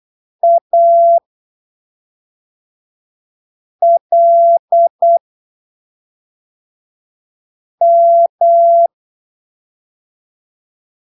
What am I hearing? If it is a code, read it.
Morse code, heard as a keyed tone, about 8 wpm, spaced as for 3 wpm, 680 Hz. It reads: ALM
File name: arquivos / texto / texto 29 08ppm